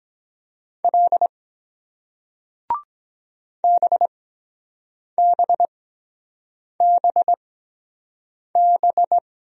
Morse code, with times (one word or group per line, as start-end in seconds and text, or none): 0.8-1.3 L
3.6-9.2 BBBB